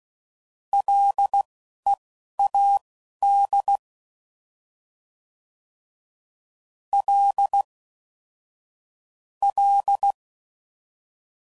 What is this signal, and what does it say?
Morse code, heard as a keyed tone, about 16 wpm, spaced as for 11 wpm, 800 Hz. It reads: LEAD L L
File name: Prefixes / FLs & THs / L Prefixes